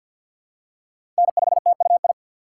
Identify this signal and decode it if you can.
Morse code, heard as a keyed tone, about 52 wpm, 700 Hz. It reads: N5TUI
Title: USA Call Signs - 5 Characters - 6x Character Spacing - 50wpm